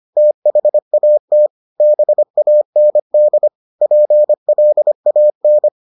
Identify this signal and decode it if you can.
Morse code, heard as a keyed tone, about 25 wpm, 600 Hz. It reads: THAT BAND PLAN